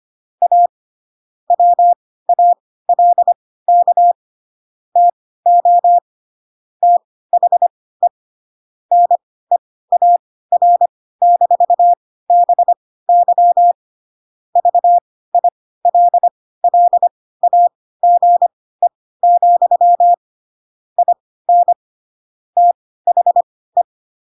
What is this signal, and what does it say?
Morse code, heard as a keyed tone, about 25 wpm, spaced as for 16 wpm, 700 Hz. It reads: A WALK TO THE NEAR-BY VILLAGE, IN THE